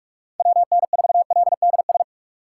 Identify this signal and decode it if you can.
Morse code, heard as a keyed tone, about 59 wpm, spaced as for 52 wpm, 700 Hz. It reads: WN4LDS
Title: Speed Racing - USA Call Signs 45wpm